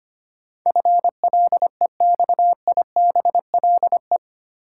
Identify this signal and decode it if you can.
Morse code, heard as a keyed tone, about 25 wpm, 700 Hz. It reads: FLEXIBLE
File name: Eight Letter Words - 25wpm